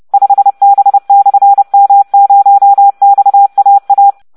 Morse code, heard as a keyed tone, about 30 wpm, 800 Hz. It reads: 5B/M0XAA